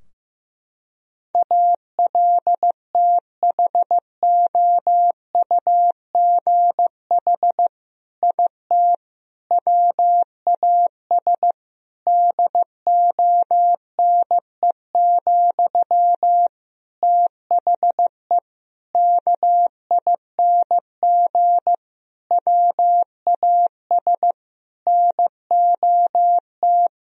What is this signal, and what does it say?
Morse code, heard as a keyed tone, about 16 wpm, 700 Hz. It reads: ALTHOUGH IT WAS DONE, THE KING WAS NOT
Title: Sentences from Top 1000 Words - Review Part 3 - 15wpm